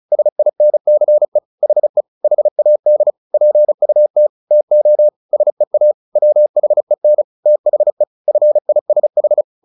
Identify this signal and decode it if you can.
Morse code, heard as a keyed tone, about 34 wpm, 600 Hz. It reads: SINCE HE HAD PUT TO SEA WHEN THE FISH
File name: the old man and the sea 2 35wpm